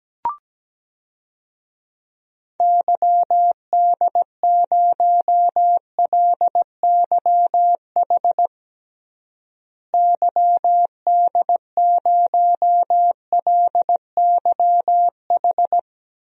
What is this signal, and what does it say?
Morse code, heard as a keyed tone, about 17 wpm, 700 Hz. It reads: YD0LYH YD0LYH
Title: DXCC-100 Call Signs - 6 Characters - Repeated 2x - 17wpm